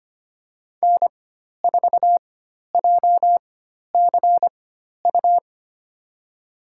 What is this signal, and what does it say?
Morse code, heard as a keyed tone, about 25 wpm, spaced as for 12 wpm, 700 Hz. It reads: N4JCU
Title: USA Call Signs - 4x Character Spacing - 25wpm